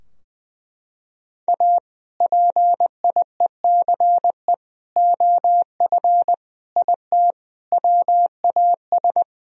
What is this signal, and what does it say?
Morse code, heard as a keyed tone, about 21 wpm, 700 Hz. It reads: A PIECE OF IT WAS
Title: Sentences from Top 1000 Words - Review Part 2 - 20wpm